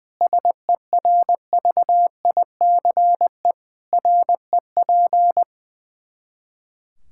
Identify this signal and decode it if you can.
Morse code, heard as a keyed tone, about 21 wpm, 700 Hz. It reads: SERVICE REP
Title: K1USN SST QSO Practice 20wpm